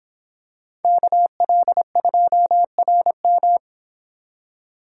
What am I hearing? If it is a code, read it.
Morse code, heard as a keyed tone, about 25 wpm, 700 Hz. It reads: KL2RM